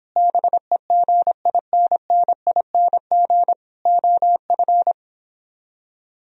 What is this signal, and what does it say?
Morse code, heard as a keyed tone, about 26 wpm, 700 Hz. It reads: BEGINNING OF